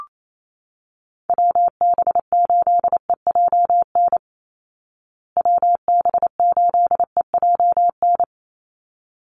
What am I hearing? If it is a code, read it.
Morse code, heard as a keyed tone, about 28 wpm, 700 Hz. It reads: WB8EJN WB8EJN
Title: Call Signs Repeated 3x - 28wpm